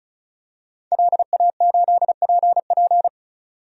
Morse code, heard as a keyed tone, about 34 wpm, 700 Hz. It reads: LA8PP
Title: DXCC-100 Call Signs - 5 Characters - 35wpm